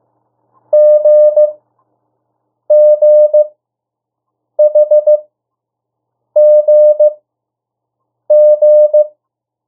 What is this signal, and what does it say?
Morse code, heard as a keyed tone, about 15 wpm, spaced as for 6 wpm, 600 Hz. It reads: GGHGG
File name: urok 1